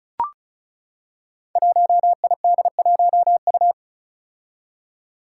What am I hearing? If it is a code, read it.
Morse code, heard as a keyed tone, about 36 wpm, 700 Hz. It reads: 1ID1U
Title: Five Challenging Letter-Number Combinations 35wpm